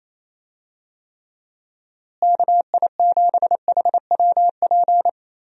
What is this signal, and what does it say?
Morse code, heard as a keyed tone, about 28 wpm, 700 Hz. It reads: KI7HWP
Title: USA Call Signs - 6 Characters - Repeated 3x - 28wpm